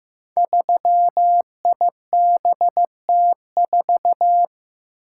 Morse code, heard as a keyed tone, about 15 wpm, 700 Hz. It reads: 3IBT4
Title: Code Groups - 5 Letter:Numbers - 15wpm